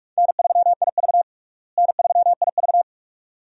Morse code, heard as a keyed tone, about 46 wpm, 700 Hz. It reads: N3IV N3IV
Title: USA Call Signs - 4 Characters - Repeated 2x - 45wpm